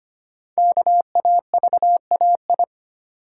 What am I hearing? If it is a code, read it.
Morse code, heard as a keyed tone, about 25 wpm, 700 Hz. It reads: KAVAI
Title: Code Groups - 5 Letters - 25wpm